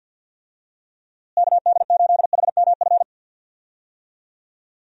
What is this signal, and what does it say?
Morse code, heard as a keyed tone, about 52 wpm, 700 Hz. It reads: KD8HMF
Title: USA Call Signs - 6 Characters - Repeated 2x - 50wpm